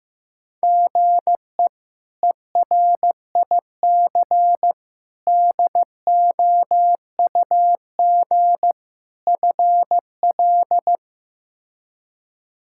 Morse code, heard as a keyed tone, about 15 wpm, 700 Hz. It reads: GE ERIC DOUG FL
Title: K1USN SST QSO Practice - 15wpm